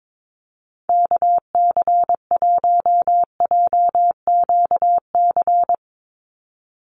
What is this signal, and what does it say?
Morse code, heard as a keyed tone, about 22 wpm, 700 Hz. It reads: KC1JQC
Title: USA Call Signs 22wpm